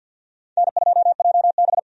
Morse code, heard as a keyed tone, about 51 wpm, 700 Hz. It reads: N2JB
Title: USA Call Signs - 4 Characters - 50wpm